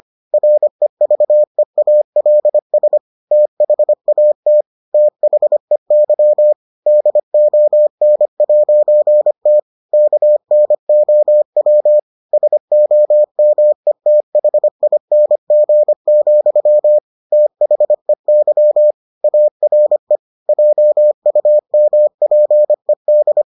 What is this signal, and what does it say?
Morse code, heard as a keyed tone, about 25 wpm, 600 Hz. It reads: REVEALS THAT THEY DON'T KNOW SOMETHING, THEY ARE JUMPED